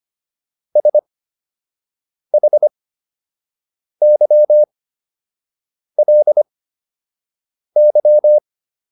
Morse code, heard as a keyed tone, about 25 wpm, spaced as for 6 wpm, 600 Hz. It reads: SHYLY